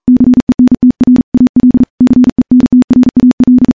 Morse code, heard as a keyed tone, about 40 wpm, 250 Hz. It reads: CENTRAL CENTRAL
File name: morse-training-40WPM